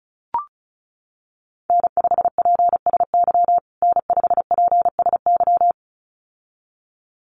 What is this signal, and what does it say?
Morse code, heard as a keyed tone, about 34 wpm, 700 Hz. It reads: N5PSY N5PSY
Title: USA Call Signs Repeated 2x 35wpm